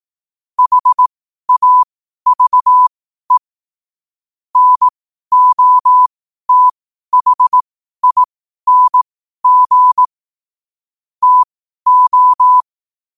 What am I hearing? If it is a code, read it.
Morse code, heard as a keyed tone, about 18 wpm, spaced as for 12 wpm, 1000 Hz. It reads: HAVE NOTHING TO
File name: qotdfiles / 12wpm